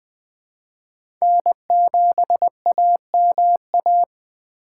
Morse code, heard as a keyed tone, about 20 wpm, 700 Hz. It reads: N7AMA
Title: 2023 SOTA Chaser Call Signs - 20wpm